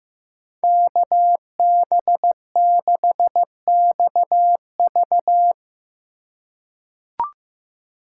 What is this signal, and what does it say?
Morse code, heard as a keyed tone, about 15 wpm, 700 Hz. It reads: KB6XV
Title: USA Call Signs - 5 Characters - 4x Character Spacing - 15wpm